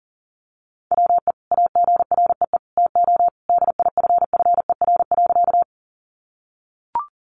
Morse code, heard as a keyed tone, about 40 wpm, 700 Hz. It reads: WE AGREE TO DIFFER.
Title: Sentences from Top 1100 Words 40wpm